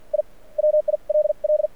Morse code, heard as a keyed tone, about 53 wpm, spaced as for 42 wpm, 600 Hz. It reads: I WILL
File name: archived / CRSnet